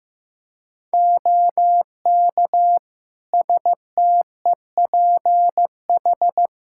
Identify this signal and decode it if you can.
Morse code, heard as a keyed tone, about 15 wpm, 700 Hz. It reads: OK STEPH
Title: Practice QSO - Exchange 2 - Info - 15wpm